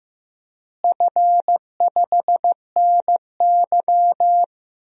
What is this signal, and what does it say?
Morse code, heard as a keyed tone, about 15 wpm, 700 Hz. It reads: F5NY
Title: DXCC-100 Call Signs - 4 Characters - Repeated 2x - 15wpm